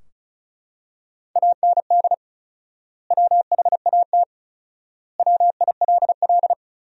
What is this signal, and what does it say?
Morse code, heard as a keyed tone, about 34 wpm, 700 Hz. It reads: AND WHAT WILL
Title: 4x Word-Spacing - Sentences from Top 200 Words - 35wpm